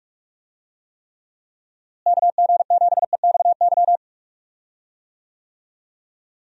Morse code, heard as a keyed tone, about 44 wpm, 700 Hz. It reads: KG7EXY
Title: USA Call Signs - 6 Characters - Repeated 2x - 45wpm